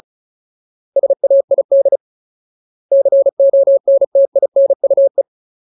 Morse code, heard as a keyed tone, about 34 wpm, 550 Hz. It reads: SAID CONTINUE
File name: Ultimate Word List / 35WPM Ultimate